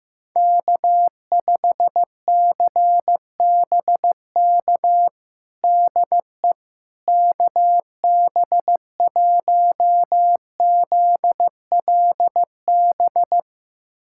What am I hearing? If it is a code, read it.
Morse code, heard as a keyed tone, about 16 wpm, 700 Hz. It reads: K5CBK DE KB1ZLB
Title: USA Callsign DE Callsign 15wpm